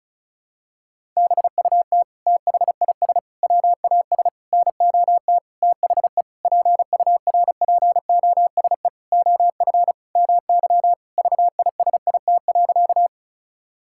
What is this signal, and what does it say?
Morse code, heard as a keyed tone, about 34 wpm, 700 Hz. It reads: BUT THIS WAS NOT THE PURPOSE OF MY VISIT.
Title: Sentences from Top 1000 Words - Review Part 6 - 35wpm